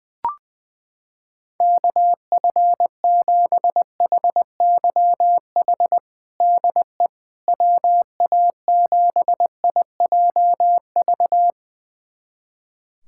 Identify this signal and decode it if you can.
Morse code, heard as a keyed tone, about 20 wpm, 700 Hz. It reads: KF7HYH DE WA7IJV
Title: USA Callsign DE Callsign - 6 Characters - 20wpm